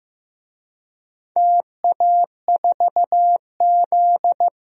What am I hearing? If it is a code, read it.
Morse code, heard as a keyed tone, about 15 wpm, 700 Hz. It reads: TA4Z
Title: DXCC-100 Call Signs - 4 Characters - Repeated 2x - 15wpm